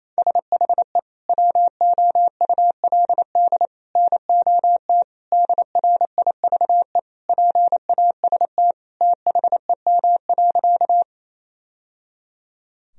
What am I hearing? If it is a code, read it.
Morse code, heard as a keyed tone, about 28 wpm, 700 Hz. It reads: SHE WOULD NOT DRIVE PAST THEM.
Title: Sentences from Top 1000 Words - Review Part 3 - 28wpm